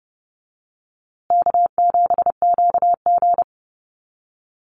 Morse code, heard as a keyed tone, about 30 wpm, 700 Hz. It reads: K7QG
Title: USA Call Signs Repeated 3x 30wpm